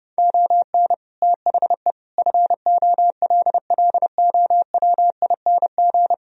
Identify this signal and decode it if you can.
Morse code, heard as a keyed tone, about 28 wpm, 700 Hz. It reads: ON THE FOLLOWING
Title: Sentences from Top 1000 Words - Review Part 4 - 30wpm